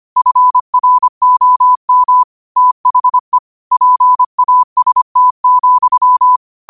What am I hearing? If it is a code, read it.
Morse code, heard as a keyed tone, about 25 wpm, 1000 Hz. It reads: FROM THE PAST,
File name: qotdfiles / 25wpm